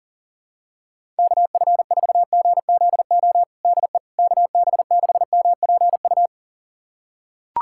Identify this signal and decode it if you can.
Morse code, heard as a keyed tone, about 40 wpm, 700 Hz. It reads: KF4GZO DE KB6MPU
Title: USA Callsign DE Callsign - 6 Characters - 40wpm